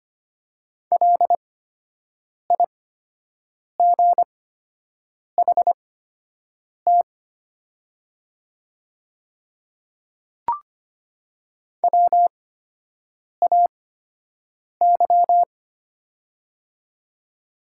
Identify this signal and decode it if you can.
Morse code, heard as a keyed tone, about 25 wpm, spaced as for 7 wpm, 690 Hz. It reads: LIGHT WAY
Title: 8x Character Spacing - Top 300 Words - 25wpm